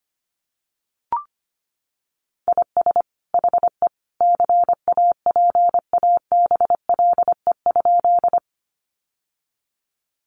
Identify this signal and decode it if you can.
Morse code, heard as a keyed tone, about 25 wpm, 700 Hz. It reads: IS HE CAPABLE?
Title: Sentences from Top 1200 Words 25wpm